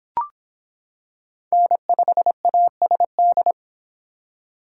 Morse code, heard as a keyed tone, about 26 wpm, 700 Hz. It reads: N5ASD